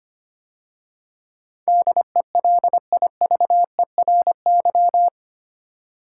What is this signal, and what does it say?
Morse code, heard as a keyed tone, about 25 wpm, 700 Hz. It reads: DELIVERY
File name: 8 Letter Words - First 3 Letters Given - 25wpm